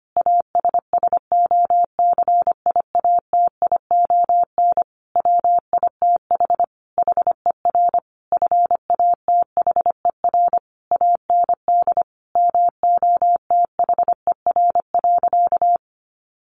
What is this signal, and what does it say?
Morse code, heard as a keyed tone, about 25 wpm, 700 Hz. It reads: ASSOCIATION WITH HER FATHER AND MOTHER.
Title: Sentences from Top 1100 Words 25wpm